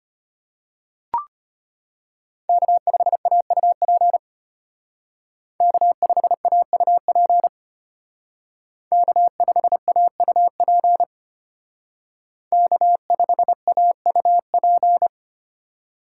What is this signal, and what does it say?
Morse code, heard as a keyed tone, about 31 wpm, 700 Hz. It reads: K5AUP K5AUP K5AUP K5AUP
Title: Speed Racing - USA 1x3 Call Signs 25wpm